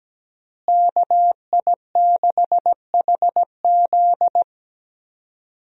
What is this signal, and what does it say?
Morse code, heard as a keyed tone, about 17 wpm, 700 Hz. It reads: KI6HZ
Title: USA Call Signs - 5 Characters - Repeated 3x - 17wpm